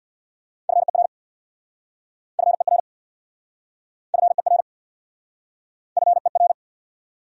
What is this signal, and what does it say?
Morse code, heard as a keyed tone, about 69 wpm, 700 Hz. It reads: FER FER FER FER